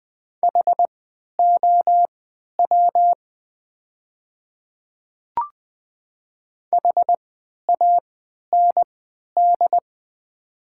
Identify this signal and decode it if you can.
Morse code, heard as a keyed tone, about 20 wpm, spaced as for 11 wpm, 700 Hz. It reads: HOW HAND
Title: 3x Character Spacing - Top 200 Words - 20wpm